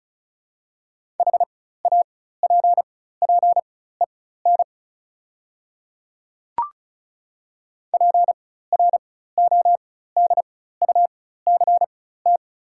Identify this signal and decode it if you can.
Morse code, heard as a keyed tone, about 34 wpm, spaced as for 16 wpm, 700 Hz. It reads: HAPPEN PRODUCT
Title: 4x Character Spacing - Top 500 Words - 35wpm